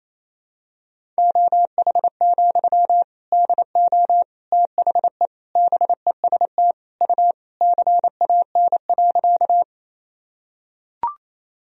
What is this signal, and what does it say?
Morse code, heard as a keyed tone, about 28 wpm, 700 Hz. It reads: OH, DO THE BEST U CAN.
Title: Sentences from Top 500 Words and Abbreviations 28wpm